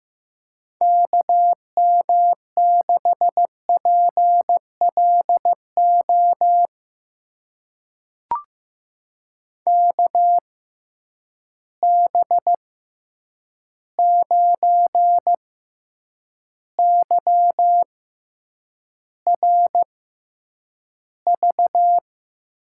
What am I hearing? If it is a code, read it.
Morse code, heard as a keyed tone, about 15 wpm, spaced as for 8 wpm, 700 Hz. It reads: KM6PLO KB9YRV
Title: USA Call Signs - 6 Characters - 6x Character Spacing - 15wpm